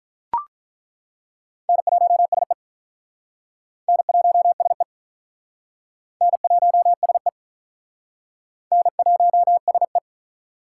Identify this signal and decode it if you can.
Morse code, heard as a keyed tone, about 47 wpm, 700 Hz. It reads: N1SE N1SE N1SE N1SE